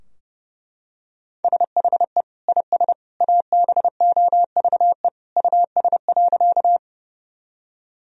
Morse code, heard as a keyed tone, about 30 wpm, 700 Hz. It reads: SHE IS ABOVE US.
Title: Sentences from Top 1000 Words - Review Part 2 - 30wpm